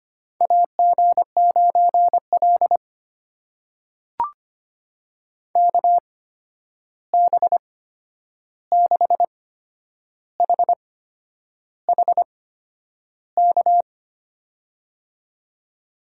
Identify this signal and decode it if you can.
Morse code, heard as a keyed tone, about 25 wpm, spaced as for 9 wpm, 700 Hz. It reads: AG9L KB6HHK
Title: USA Call Signs - 8x Character Spacing - 25wpm